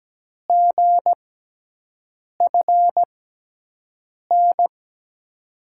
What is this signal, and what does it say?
Morse code, heard as a keyed tone, about 17 wpm, spaced as for 6 wpm, 700 Hz. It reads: GFN
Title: USA Call Signs - 6 Characters - 6x Character Spacing - 17wpm